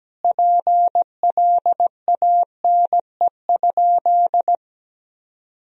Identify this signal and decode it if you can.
Morse code, heard as a keyed tone, about 17 wpm, 700 Hz. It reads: PLANE?
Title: Sentences from Top 1000 Words - Review Part 3 - 17wpm